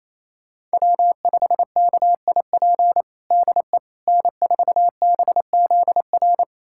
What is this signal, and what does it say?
Morse code, heard as a keyed tone, about 28 wpm, 700 Hz. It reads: W5KIP DE N4BZR
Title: USA Callsign DE Callsign - 5 Characters - 28wpm